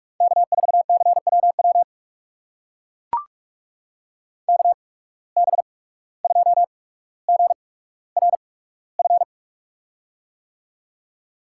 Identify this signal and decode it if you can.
Morse code, heard as a keyed tone, about 44 wpm, spaced as for 18 wpm, 700 Hz. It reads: K4CWW KB2GRF
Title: USA Call Signs - 8x Character Spacing - 45wpm